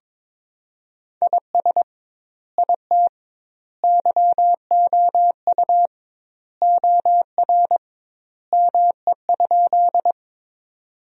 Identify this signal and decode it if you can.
Morse code, heard as a keyed tone, about 22 wpm, 700 Hz. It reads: IS IT YOU OR ME?
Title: 2x Word-Spacing - Sentences from Top 200 Words - 22wpm